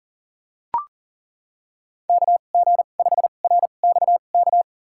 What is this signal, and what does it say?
Morse code, heard as a keyed tone, about 40 wpm, spaced as for 29 wpm, 700 Hz. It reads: KG5RXK
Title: USA Call Signs - 6 Characters - 2x Character Spacing - 40wpm